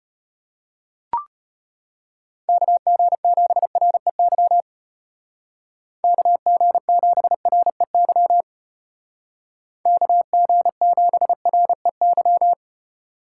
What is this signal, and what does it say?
Morse code, heard as a keyed tone, about 31 wpm, 700 Hz. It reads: KG7REY KG7REY KG7REY
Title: Speed Racing - USA 2x3 Call Signs 25wpm